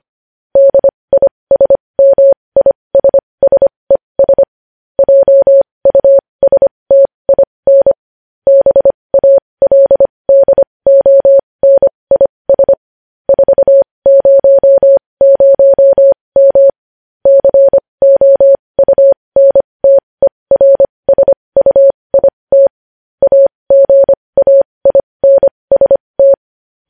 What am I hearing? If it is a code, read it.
Morse code, heard as a keyed tone, about 25 wpm, spaced as for 20 wpm, 550 Hz. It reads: DISMISSES JUSTIN BALDONIS 400M COUNTERSUIT AGAINST
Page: news-headlines-in-morse-code-at-20-wpm